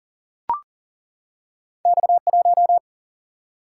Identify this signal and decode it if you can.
Morse code, heard as a keyed tone, about 42 wpm, 700 Hz. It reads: X 1
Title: Single-Letter Single-Number 40wpm